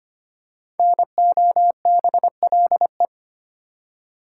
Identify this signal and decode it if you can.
Morse code, heard as a keyed tone, about 25 wpm, 700 Hz. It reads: NOBLE
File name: 5 Letter Words - First 3 Letters Given - 25wpm